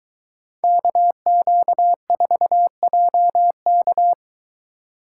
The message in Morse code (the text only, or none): KQ4JK